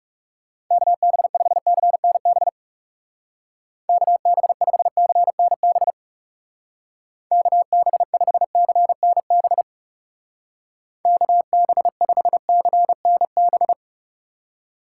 Morse code, heard as a keyed tone, about 35 wpm, 700 Hz. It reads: KB5CNB KB5CNB KB5CNB KB5CNB
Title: Speed Racing - USA 2x3 Call Signs 30wpm